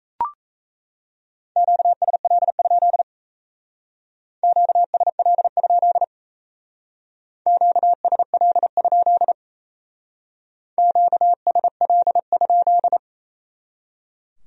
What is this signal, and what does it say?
Morse code, heard as a keyed tone, about 36 wpm, 700 Hz. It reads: QSL? QSL? QSL? QSL?